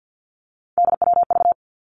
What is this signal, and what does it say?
Morse code, heard as a keyed tone, about 50 wpm, 700 Hz. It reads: DWV
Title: Three Challenging Letter-Number Combinations 50wpm